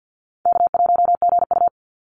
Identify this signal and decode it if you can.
Morse code, heard as a keyed tone, about 48 wpm, 700 Hz. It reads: K1GU
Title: Easy Call Signs Repeated 3x - 50wpm